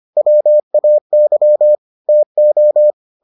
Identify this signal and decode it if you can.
Morse code, heard as a keyed tone, about 25 wpm, 600 Hz. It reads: WAY TO